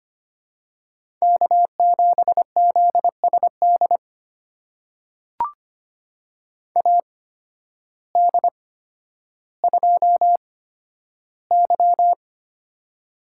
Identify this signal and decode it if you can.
Morse code, heard as a keyed tone, about 25 wpm, spaced as for 12 wpm, 700 Hz. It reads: K7ZSD AD2Y
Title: USA Call Signs - 8x Character Spacing - 25wpm